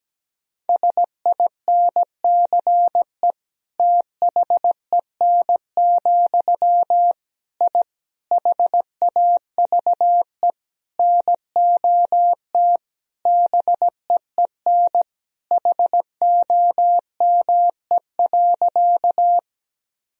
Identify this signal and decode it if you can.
Morse code, heard as a keyed tone, about 17 wpm, 700 Hz. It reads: SINCE THEN, I HAVE NOT BEEN HOME.
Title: Sentences from Top 1000 Words - Review Part 1 - 17wpm